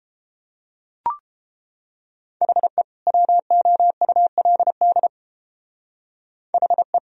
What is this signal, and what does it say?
Morse code, heard as a keyed tone, about 30 wpm, 700 Hz. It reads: HE WOULD HE